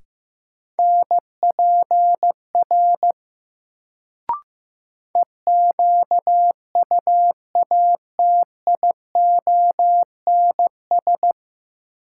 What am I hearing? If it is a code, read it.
Morse code, heard as a keyed tone, about 15 wpm, 700 Hz. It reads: NPR EQUATIONS
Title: Sets of 5 Words - New Word Drill - 15wpm